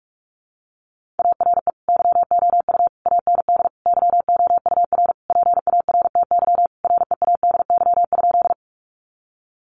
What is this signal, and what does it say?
Morse code, heard as a keyed tone, about 44 wpm, 700 Hz. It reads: ARE YOU AND YOUR PARTY READY?
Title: Sentences from Top 1400 Words 45wpm